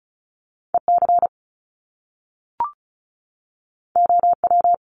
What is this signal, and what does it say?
Morse code, heard as a keyed tone, about 36 wpm, 700 Hz. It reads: EC OW
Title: Top Two Letter Combinations in Words 35wpm